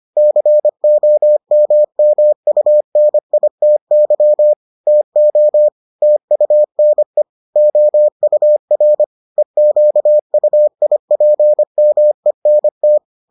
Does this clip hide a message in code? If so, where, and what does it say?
Yes: Morse code from 0.2 to 13.0 s, heard as a keyed tone, about 25 wpm, 600 Hz. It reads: COMMUNITY TO TUNE OUR EQUIPMENT